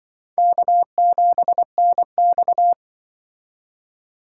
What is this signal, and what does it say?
Morse code, heard as a keyed tone, about 24 wpm, 700 Hz. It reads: K7NX
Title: Speed Racing - USA Call Signs 20wpm